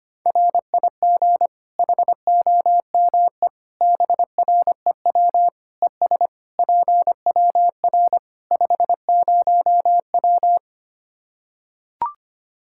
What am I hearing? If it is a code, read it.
Morse code, heard as a keyed tone, about 25 wpm, 700 Hz. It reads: RIG HOME BREW ES PWR 50W
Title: Practice QSO - Exchange 2 - Info - 25wpm